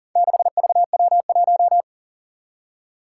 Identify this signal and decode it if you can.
Morse code, heard as a keyed tone, about 40 wpm, 700 Hz. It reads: 6VW1
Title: Four Challenging Letter-Number Combinations 40wpm